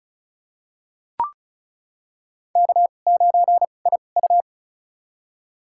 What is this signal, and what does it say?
Morse code, heard as a keyed tone, about 36 wpm, spaced as for 25 wpm, 690 Hz. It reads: K9IU
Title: USA Call Signs - 4 Characters - 2x Character Spacing - 35wpm